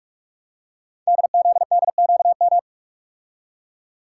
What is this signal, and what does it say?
Morse code, heard as a keyed tone, about 44 wpm, 690 Hz. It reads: DZDQM
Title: Code Groups - 5 Letters - 45wpm